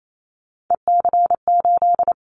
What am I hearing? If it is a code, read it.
Morse code, heard as a keyed tone, about 28 wpm, 700 Hz. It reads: EC8
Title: Callsign Prefixes for Top 50 DXCC - 28wpm